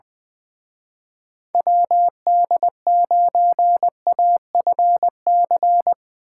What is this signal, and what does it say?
Morse code, heard as a keyed tone, about 22 wpm, 700 Hz. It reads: WD9AFC
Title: Speed Racing - USA 2x3 Call Signs 20wpm